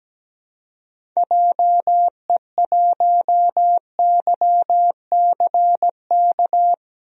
Morse code, heard as a keyed tone, about 17 wpm, 700 Hz. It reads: JE1YCK
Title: DXCC-100 Call Signs - 6 Characters - Repeated 3x - 17wpm